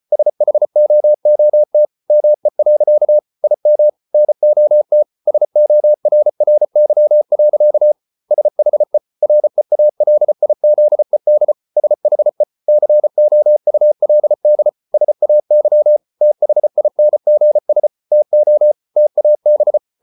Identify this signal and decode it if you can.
Morse code, heard as a keyed tone, about 34 wpm, 600 Hz. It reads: SHOOT ME. IM NOT SORRY. SHE REALIZED SHE COULD SAY THINGS TO TAB